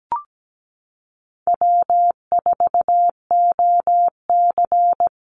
Morse code, heard as a keyed tone, about 17 wpm, 700 Hz. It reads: W4OC
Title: Easy Call Signs Repeated 3x - 17wpm